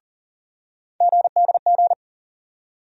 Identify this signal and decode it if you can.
Morse code, heard as a keyed tone, about 40 wpm, 700 Hz. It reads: GDG
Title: USA Call Sign Postfixes - 3 Letters 40wpm